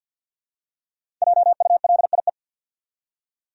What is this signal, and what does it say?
Morse code, heard as a keyed tone, about 46 wpm, 700 Hz. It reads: JULIE
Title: USA Common Names - Max 5 Letters - 50wpm